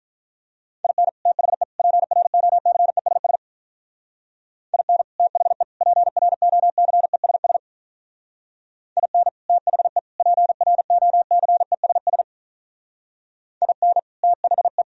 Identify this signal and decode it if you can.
Morse code, heard as a keyed tone, about 42 wpm, 690 Hz. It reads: IN THE PROCESS IN THE PROCESS IN THE PROCESS IN THE
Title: Speed-Racing - Sets of 3 Words 35wpm